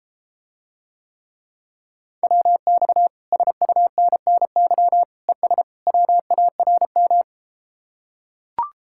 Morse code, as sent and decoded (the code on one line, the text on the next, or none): .-- -..- / ... ..- -. -. -.-- / . ... / .-- .- .-. --
WX SUNNY ES WARM